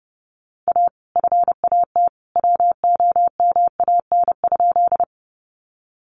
Morse code, heard as a keyed tone, about 30 wpm, 700 Hz. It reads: A FAT WOMAN?
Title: Sentences from Top 1100 Words 30wpm